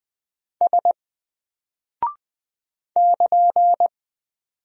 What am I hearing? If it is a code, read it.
Morse code, heard as a keyed tone, about 20 wpm, 700 Hz. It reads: S <KN>
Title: Single Character (Letter:Number:Punctuation:Prosign) Alt 20wpm